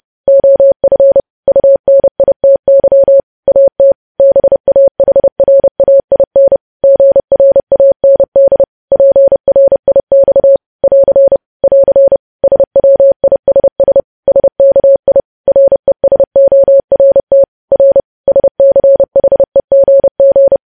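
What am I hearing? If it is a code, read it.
Morse code, heard as a keyed tone, about 30 wpm, 550 Hz. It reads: OF UNITY AT BAHRAIN GRAND PRIX <AR> <AR> SWISS SKI RESORT R SCHEGG
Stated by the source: Morse code practice of news headlines at 30.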